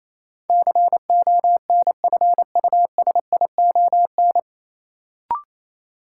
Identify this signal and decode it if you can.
Morse code, heard as a keyed tone, about 28 wpm, 700 Hz. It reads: CONFUSION
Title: Sentences from Top 1600 Words - New Word Drill - 28wpm